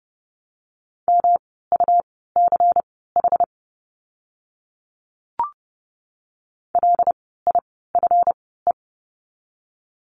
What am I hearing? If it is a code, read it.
Morse code, heard as a keyed tone, about 30 wpm, spaced as for 17 wpm, 700 Hz. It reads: MUCH LIFE
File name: course / 30:10wpm / 237 Lesson 38f - Letter q - Words 30wpm